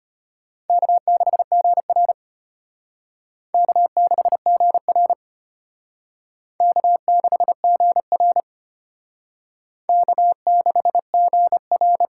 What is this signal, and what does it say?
Morse code, heard as a keyed tone, about 33 wpm, 700 Hz. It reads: K6GR K6GR K6GR K6GR